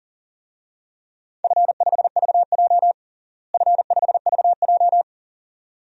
Morse code, heard as a keyed tone, about 37 wpm, 690 Hz. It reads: F5VJ F5VJ
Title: DXCC-100 Call Signs - 4 Characters - Repeated 2x - 40wpm